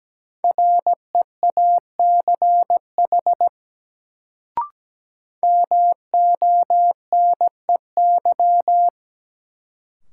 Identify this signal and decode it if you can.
Morse code, heard as a keyed tone, about 17 wpm, 700 Hz. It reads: REACH MONEY
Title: Sentences from Top 400 Words - New Word Drill - 17wpm